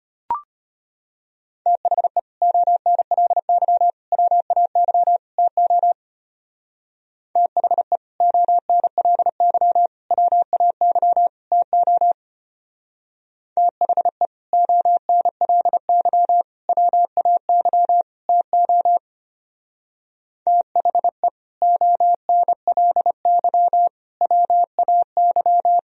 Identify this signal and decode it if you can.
Morse code, heard as a keyed tone, about 32 wpm, 700 Hz. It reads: THE ONLY WAY TO THE ONLY WAY TO THE ONLY WAY TO THE ONLY WAY